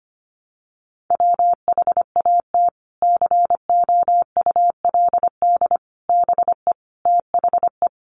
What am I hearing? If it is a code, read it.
Morse code, heard as a keyed tone, about 25 wpm, 700 Hz. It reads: WHAT COULD BE THE
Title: Sentences from Top 900 Words 25wpm